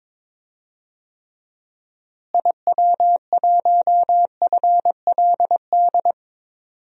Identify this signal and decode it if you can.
Morse code, heard as a keyed tone, about 22 wpm, 700 Hz. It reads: IW1FLD
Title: DXCC-100 Call Signs - 6 Characters - Repeated 3x - 22wpm